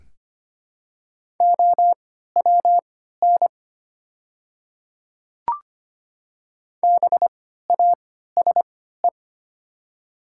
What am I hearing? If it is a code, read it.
Morse code, heard as a keyed tone, about 25 wpm, spaced as for 14 wpm, 700 Hz. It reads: OWN BASE
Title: 3x Character Spacing - Top 300 Words - 25wpm